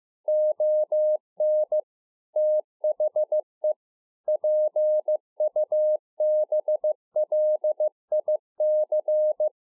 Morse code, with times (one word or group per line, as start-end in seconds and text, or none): 0.3-1.8 ON
2.4-3.7 THE
4.3-9.5 PUBLIC